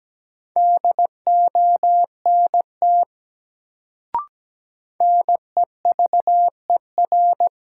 Morse code, heard as a keyed tone, about 17 wpm, 700 Hz. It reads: DONT NEVER
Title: Sentences from Top 300 Words - New Word Drill - 17wpm